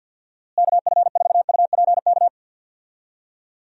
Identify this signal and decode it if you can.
Morse code, heard as a keyed tone, about 49 wpm, 700 Hz. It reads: KF4UPK
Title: USA Call Signs - 6 Characters - Repeated 3x - 50wpm